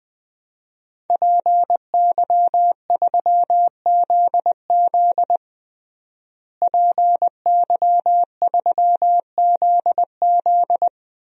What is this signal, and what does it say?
Morse code, heard as a keyed tone, about 20 wpm, 700 Hz. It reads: PY3ZZ PY3ZZ
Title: DXCC-100 Call Signs - 5 Characters - Repeated 2x - 20wpm